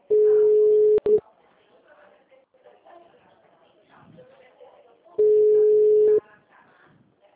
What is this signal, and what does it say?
Ringback tone (Europe, 425 Hz), with speech.
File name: out- -2002- - - .24853